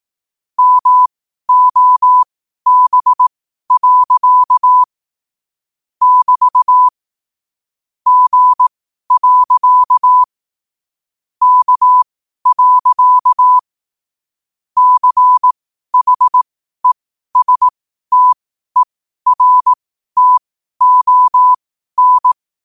Morse code, heard as a keyed tone, about 18 wpm, spaced as for 12 wpm, 1000 Hz. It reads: MOB. <BT> G. K. CHESTERTON
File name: qotdfiles / 12wpm